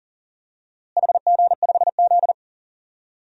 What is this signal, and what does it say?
Morse code, heard as a keyed tone, about 39 wpm, 700 Hz. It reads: HG5Z
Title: DXCC-100 Call Signs - 4 Characters - Repeated 2x - 40wpm